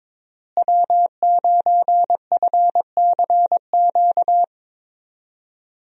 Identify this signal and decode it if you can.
Morse code, heard as a keyed tone, about 22 wpm, 700 Hz. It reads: W9FCQ
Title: USA Call Signs - 5 Characters - Repeated 3x - 22wpm